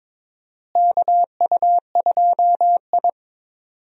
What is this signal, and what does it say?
Morse code, heard as a keyed tone, about 22 wpm, 700 Hz. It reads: KU2I
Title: USA Call Signs - 4 Characters - Repeated 2x - 22wpm